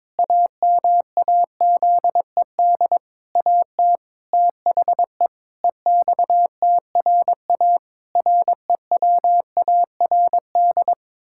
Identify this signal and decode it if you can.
Morse code, heard as a keyed tone, about 22 wpm, 700 Hz. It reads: AMAZED AT THE EXTRA REWARD
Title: Pangram Challenge v2 - 22wpm